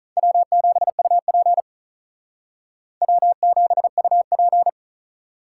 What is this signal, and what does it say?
Morse code, heard as a keyed tone, about 44 wpm, 700 Hz. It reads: W7UP W7UP